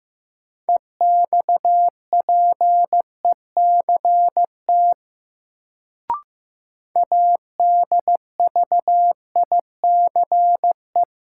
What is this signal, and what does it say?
Morse code, heard as a keyed tone, about 15 wpm, 700 Hz. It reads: EXPECT ADVICE
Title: Sentences from Top 1100 Words - New Word Drill Alt - 15wpm